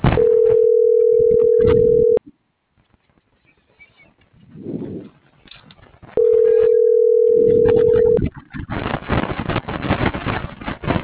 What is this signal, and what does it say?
Ringback tone (North America), 440 + 480 Hz.